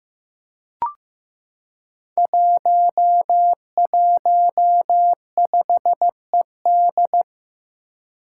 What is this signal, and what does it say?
Morse code, heard as a keyed tone, about 15 wpm, 700 Hz. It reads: 115ED
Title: Five Challenging Letter-Number Combinations 15wpm